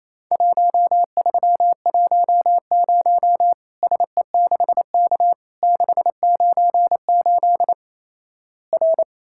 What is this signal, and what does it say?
Morse code, heard as a keyed tone, about 28 wpm, 700 Hz. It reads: 1310 SE6K 698 R
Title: Worked All Europe DX Contest - QTC Exchange - 28wpm